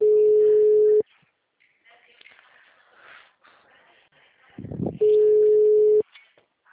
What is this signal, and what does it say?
Ringback tone (Europe, 425 Hz), with speech.